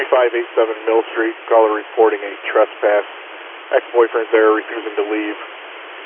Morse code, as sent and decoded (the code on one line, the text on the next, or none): none